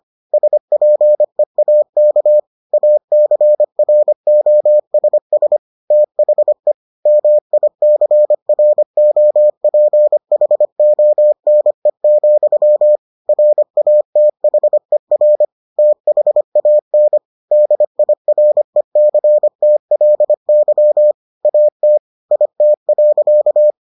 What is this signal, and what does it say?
Morse code, heard as a keyed tone, about 25 wpm, 600 Hz. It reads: SPEAK ACROSS THE MICROPHONE, RATHER THAN DIRECTLY AT IT.